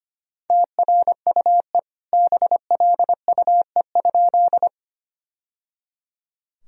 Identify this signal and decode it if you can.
Morse code, heard as a keyed tone, about 25 wpm, 700 Hz. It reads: TRUE BLUE?
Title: Sentences from Top 1000 Words - Review Part 3 - 25wpm